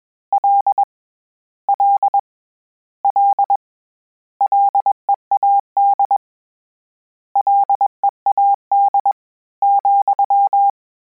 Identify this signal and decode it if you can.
Morse code, heard as a keyed tone, about 21 wpm, 800 Hz. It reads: L L L LEAD LEAD,
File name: Combinations / Combinations-mp3s / Combinations@20